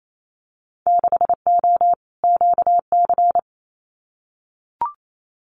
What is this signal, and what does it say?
Morse code, heard as a keyed tone, about 28 wpm, 700 Hz. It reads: 6O QC
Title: Winter Field Day Exchanges Encore 28wpm